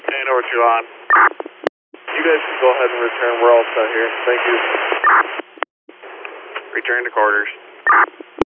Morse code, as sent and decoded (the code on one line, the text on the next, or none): none